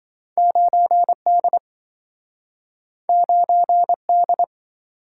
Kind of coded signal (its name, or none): Morse code